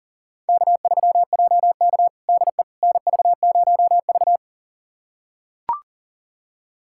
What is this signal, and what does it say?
Morse code, heard as a keyed tone, about 37 wpm, 700 Hz. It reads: K3JK DE NV0V
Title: USA Callsign DE Callsign - 4 Characters - 40wpm